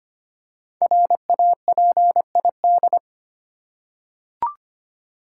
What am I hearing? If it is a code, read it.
Morse code, heard as a keyed tone, about 25 wpm, 700 Hz. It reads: RAPID
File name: 5 Letter Words - First 3 Letters Given - 25wpm